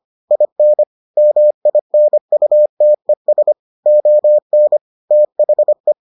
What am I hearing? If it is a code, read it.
Morse code, heard as a keyed tone, about 25 wpm, 600 Hz. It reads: IN MINUTES ON THE